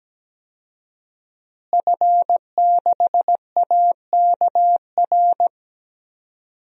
Morse code, heard as a keyed tone, about 17 wpm, 700 Hz. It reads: F6AKR
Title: DXCC-100 Call Signs - 5 Characters - Repeated 2x - 17wpm